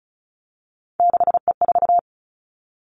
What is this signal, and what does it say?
Morse code, heard as a keyed tone, about 36 wpm, 700 Hz. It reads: 6E4
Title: Three Challenging Letter-Number Combinations 35wpm